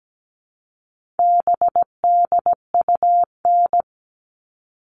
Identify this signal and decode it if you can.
Morse code, heard as a keyed tone, about 17 wpm, 700 Hz. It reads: BDUN
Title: All Cut Numbers - 4-Digit 17wpm